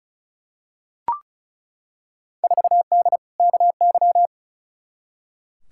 Morse code, heard as a keyed tone, about 34 wpm, 700 Hz. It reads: 4D KY
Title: Field Day Exchanges Encore - 35wpm